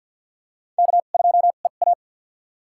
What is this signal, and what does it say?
Morse code, heard as a keyed tone, about 52 wpm, spaced as for 36 wpm, 700 Hz. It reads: K2EA
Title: USA Call Signs - 4 Characters - 2x Character Spacing - 50wpm